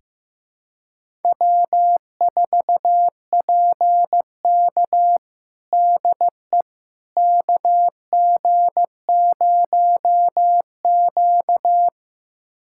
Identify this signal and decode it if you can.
Morse code, heard as a keyed tone, about 15 wpm, 700 Hz. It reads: W4PK DE KG0Q
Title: USA Callsign DE Callsign - 4 Characters - 15wpm